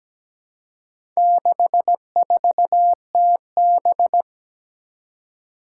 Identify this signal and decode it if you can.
Morse code, heard as a keyed tone, about 17 wpm, 700 Hz. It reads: 64TB
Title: Four Challenging Letter-Number Combinations 17wpm